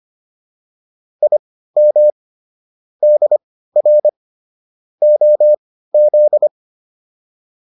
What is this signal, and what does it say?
Morse code, heard as a keyed tone, about 25 wpm, spaced as for 15 wpm, 600 Hz. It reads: IM DR OZ